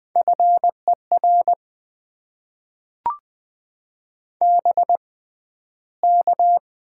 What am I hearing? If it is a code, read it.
Morse code, heard as a keyed tone, about 20 wpm, spaced as for 12 wpm, 700 Hz. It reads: FER BK
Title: Top 90 QSO Elements - 6x Character Spacing - 20wpm